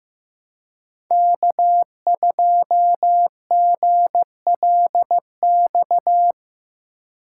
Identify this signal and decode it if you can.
Morse code, heard as a keyed tone, about 15 wpm, 700 Hz. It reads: K2GLX
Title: USA Call Signs - 2x Character Spacing - 15wpm